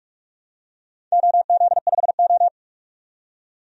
Morse code, heard as a keyed tone, about 46 wpm, 700 Hz. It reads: OZ5O
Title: DXCC-100 Call Signs - 4 Characters - 45wpm